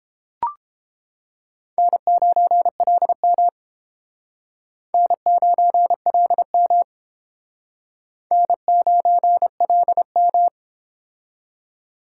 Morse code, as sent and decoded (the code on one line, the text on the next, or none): -. ----. .-.. -- / -. ----. .-.. -- / -. ----. .-.. --
N9LM N9LM N9LM